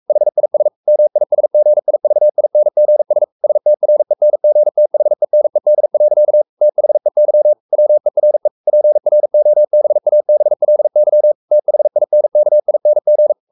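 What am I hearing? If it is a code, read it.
Morse code, heard as a keyed tone, about 44 wpm, 600 Hz. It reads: HIS MISGIVINGS STRENGTHENED. THEY WERE PROBABLY THINKING
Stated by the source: Morse code at 43 wpm sending Ken Follett's novel Never in English, from CW med Gnister.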